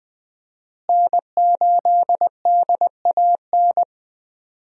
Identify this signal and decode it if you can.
Morse code, heard as a keyed tone, about 20 wpm, 700 Hz. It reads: N8DAN
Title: USA Call Signs - 5 Characters - Repeated 2x - 20wpm